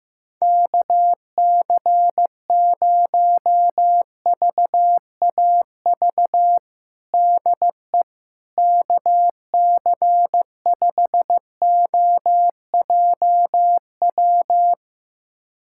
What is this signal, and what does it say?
Morse code, heard as a keyed tone, about 16 wpm, 700 Hz. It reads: KC0VAV DE KC5OJW
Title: USA Callsign DE Callsign 15wpm